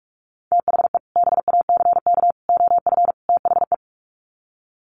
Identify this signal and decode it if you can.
Morse code, heard as a keyed tone, about 44 wpm, 700 Hz. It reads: THE BACK OF THE
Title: Sets of 4 Words 45wpm